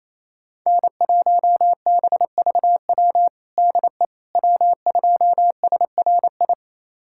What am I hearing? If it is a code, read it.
Morse code, heard as a keyed tone, about 28 wpm, 700 Hz. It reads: N1BVW DE W2SRI
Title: USA Callsign DE Callsign - 5 Characters - 28wpm